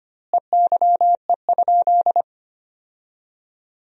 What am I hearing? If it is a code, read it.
Morse code, heard as a keyed tone, about 25 wpm, 700 Hz. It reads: EYE?
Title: 2x Word-Spacing - Sentences from Top 300 Words - 25wpm